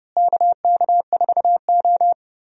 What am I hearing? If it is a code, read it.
Morse code, heard as a keyed tone, about 30 wpm, 700 Hz. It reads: KK4O
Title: USA Call Signs - 4 Characters - Repeated 2x - 30wpm